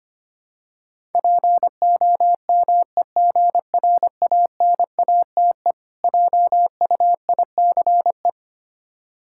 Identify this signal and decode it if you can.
Morse code, heard as a keyed tone, about 25 wpm, 700 Hz. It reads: POMEGRANATE JUICE
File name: practice / 25wpm / Difficult Words with Hints - 25wpm